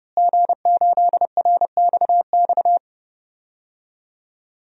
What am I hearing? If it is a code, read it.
Morse code, heard as a keyed tone, about 30 wpm, 700 Hz. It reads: G8RXX
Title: DXCC-100 Call Signs - 5 Characters - Repeated 2x - 30wpm